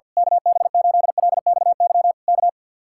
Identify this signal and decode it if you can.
Morse code, heard as a keyed tone, about 49 wpm, 700 Hz. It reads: KB8LXY K
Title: Speed-Racing Common QSO Phrases - 50wpm